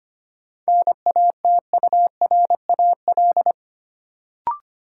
Morse code, heard as a keyed tone, about 25 wpm, 700 Hz. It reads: NATURAL
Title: Sentences from Top 700 Words - New Word Drill - 25wpm